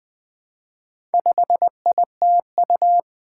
Morse code, heard as a keyed tone, about 20 wpm, 700 Hz. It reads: 5ITU
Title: Four Challenging Letter-Number Combinations 20wpm